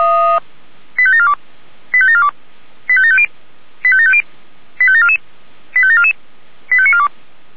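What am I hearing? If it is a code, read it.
Five-tone selective call (ZVEI), heard as quick runs of five tones, tones of about 70 ms. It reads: 85742 85742 85750 85750 85744 85744 85942
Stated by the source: ZVEI five-tone selective call received on SDR.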